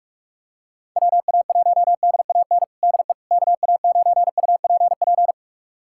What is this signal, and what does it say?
Morse code, heard as a keyed tone, about 43 wpm, 700 Hz. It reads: WA1DAN DE KA9UPP
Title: USA Callsign DE Callsign 45wpm